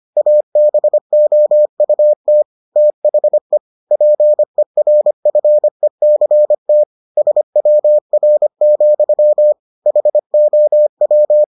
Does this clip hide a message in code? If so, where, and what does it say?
Yes: Morse code from 0.2 to 11.4 s, heard as a keyed tone, about 25 wpm, 600 Hz. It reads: ABOUT THE PERFECT SWR, HOW